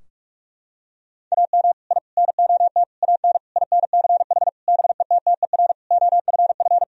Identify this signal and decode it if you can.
Morse code, heard as a keyed tone, about 46 wpm, 700 Hz. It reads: AM I NOT AN INCH BETTER OFF
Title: Sentences from Top 1000 Words - Review Part 3 - 45wpm